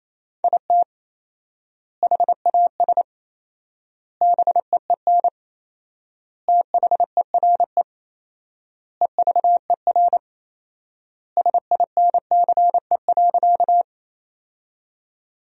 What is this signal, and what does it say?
Morse code, heard as a keyed tone, about 27 wpm, 700 Hz. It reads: IT HAS BEEN THERE EVER SINCE.
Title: 4x Word-Spacing - Sentences from Top 400 Words - 28wpm